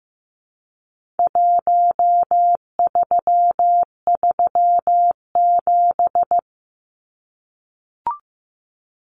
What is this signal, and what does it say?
Morse code, heard as a keyed tone, about 15 wpm, 700 Hz. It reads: 1337
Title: Numbers - 4-Digit 15wpm